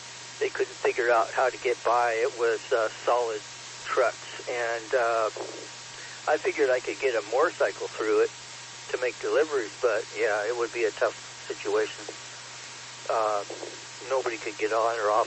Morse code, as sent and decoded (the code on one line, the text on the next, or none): none